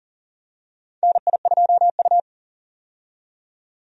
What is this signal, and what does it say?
Morse code, heard as a keyed tone, about 40 wpm, 700 Hz. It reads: NI2U
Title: USA Call Signs - 4 Characters - 40wpm